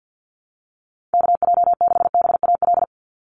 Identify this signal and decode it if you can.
Morse code, heard as a keyed tone, about 50 wpm, 700 Hz. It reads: KJ6BAL
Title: USA Call Signs 50wpm